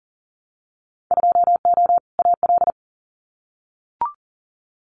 Morse code, heard as a keyed tone, about 42 wpm, 700 Hz. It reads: 2O AL
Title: Winter Field Day Exchanges 40wpm